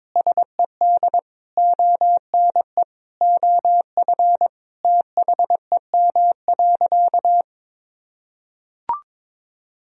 Morse code, heard as a keyed tone, about 22 wpm, 700 Hz. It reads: SED ONE OF THEM.
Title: Sentences from Top 300 Words and Abbreviations 22wpm